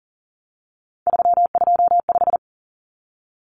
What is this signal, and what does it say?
Morse code, heard as a keyed tone, about 40 wpm, 700 Hz. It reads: 325
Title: Common Cut Numbers - 3-Digit 40wpm